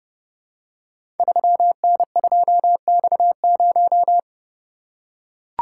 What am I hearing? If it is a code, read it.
Morse code, heard as a keyed tone, about 30 wpm, 700 Hz. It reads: 3N2X0
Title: Code Groups - 5 Letter:Numbers - 30wpm